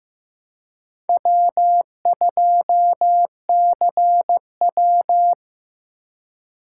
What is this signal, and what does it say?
Morse code, heard as a keyed tone, about 18 wpm, 700 Hz. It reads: W2CW
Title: Speed Racing - USA 1x2 Call Signs 15wpm